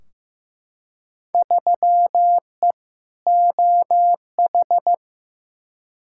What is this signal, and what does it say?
Morse code, heard as a keyed tone, about 15 wpm, 700 Hz. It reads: 3E OH
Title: Field Day Exchanges - 15wpm